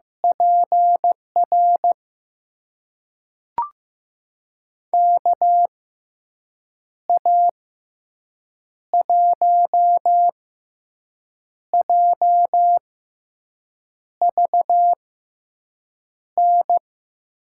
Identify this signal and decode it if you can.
Morse code, heard as a keyed tone, about 15 wpm, spaced as for 6 wpm, 700 Hz. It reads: PR KA1JVN
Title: USA Call Signs - 6 Characters - 6x Character Spacing - 15wpm